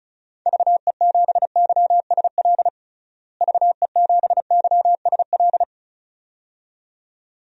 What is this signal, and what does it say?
Morse code, heard as a keyed tone, about 34 wpm, 700 Hz. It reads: VE7YSL VE7YSL
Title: DXCC-100 Call Signs - 6 Characters - Repeated 2x - 35wpm